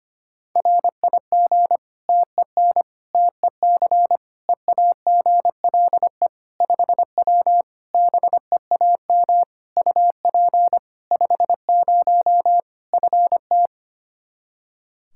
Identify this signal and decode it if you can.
Morse code, heard as a keyed tone, about 25 wpm, 700 Hz. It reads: RIG TEN TEC EAGLE 5W BEAM UP 50 FT
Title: Practice QSO - Exchange 2 - Info - 25wpm